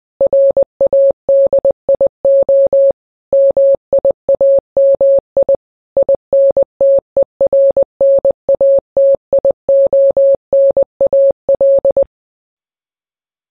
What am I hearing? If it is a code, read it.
Morse code, heard as a keyed tone, about 20 wpm, 550 Hz. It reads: RADIO MIAMI INTERNATIONAL
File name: radio / Imaginary Stations / KSPY 9395 kHz 29.1.23